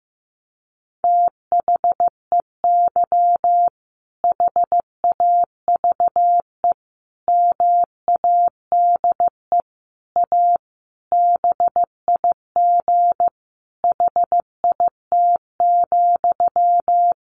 Morse code, heard as a keyed tone, about 15 wpm, 700 Hz. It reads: THEY HAVE MADE A BIG HIT,
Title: Sentences from Top 1200 Words 15wpm